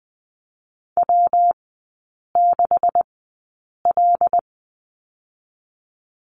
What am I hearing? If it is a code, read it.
Morse code, heard as a keyed tone, about 20 wpm, spaced as for 8 wpm, 700 Hz. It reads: W6L
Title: 127 Lesson 21e - Number 6 - Three Characters 20wpm